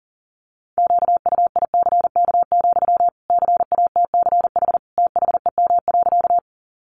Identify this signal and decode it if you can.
Morse code, heard as a keyed tone, about 37 wpm, 700 Hz. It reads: QUICK, CATCH THEM.
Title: Sentences from Top 1500 Words 40wpm